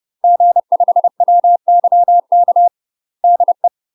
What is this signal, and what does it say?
Morse code, heard as a keyed tone, about 30 wpm, 700 Hz. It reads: G5WYK DE
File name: UK MASTER QSOs TRAINING / 30WPM / UK MASTER QSOs 0020